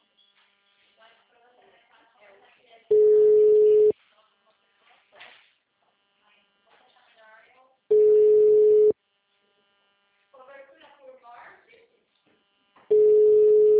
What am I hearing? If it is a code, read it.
Ringback tone (Europe, 425 Hz), with speech.
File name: out- -2005- - - .90